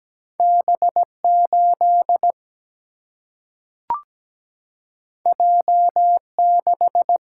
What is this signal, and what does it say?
Morse code, heard as a keyed tone, about 17 wpm, 700 Hz. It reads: B8 J6
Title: Single-Letter Single-Number 17wpm